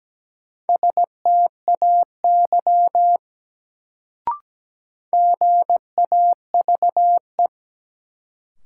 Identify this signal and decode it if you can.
Morse code, heard as a keyed tone, about 17 wpm, 700 Hz. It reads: STAY GAVE
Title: Sentences from Top 500 Words - New Word Drill - 17wpm